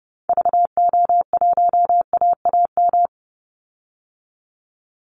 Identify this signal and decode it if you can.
Morse code, heard as a keyed tone, about 29 wpm, 700 Hz. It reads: VO1AAM
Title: Call Signs Repeated 3x - 30wpm